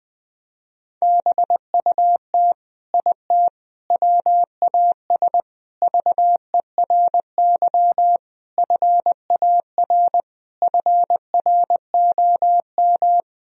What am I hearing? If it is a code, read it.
Morse code, heard as a keyed tone, about 20 wpm, 700 Hz. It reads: BUT IT WAS VERY FAR FROM
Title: Sentences from Top 1000 Words - Review Part 2 - 20wpm